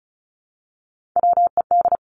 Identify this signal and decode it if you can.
Morse code, heard as a keyed tone, about 36 wpm, 700 Hz. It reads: WED
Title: Three Challenging Letter-Number Combinations 35wpm